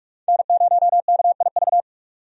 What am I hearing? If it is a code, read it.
Morse code, heard as a keyed tone, about 43 wpm, 700 Hz. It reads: N0KIV
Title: USA Call Signs - 5 Characters - 45wpm